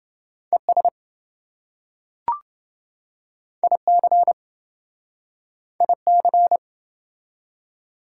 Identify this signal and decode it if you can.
Morse code, heard as a keyed tone, about 25 wpm, 700 Hz. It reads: ES IC IC
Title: Speed-Racing Prefixes and Suffixes 20wpm